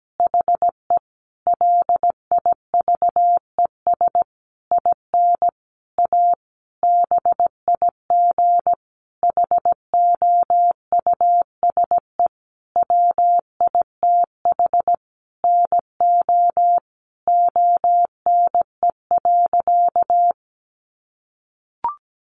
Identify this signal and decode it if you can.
Morse code, heard as a keyed tone, about 17 wpm, 700 Hz. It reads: HE LIVES IN A BIG HOUSE WITH NO ONE.
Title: Additional Easy Phrases 17wpm